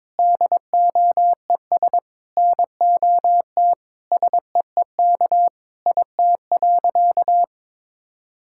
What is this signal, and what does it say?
Morse code, heard as a keyed tone, about 22 wpm, 700 Hz. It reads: DOES NOT SEEK IT.
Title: Sentences from Top 1600 Words 22wpm